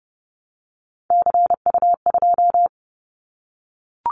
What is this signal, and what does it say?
Morse code, heard as a keyed tone, about 30 wpm, 700 Hz. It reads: CU2
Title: Callsign Prefixes for Top 50 DXCC - 30wpm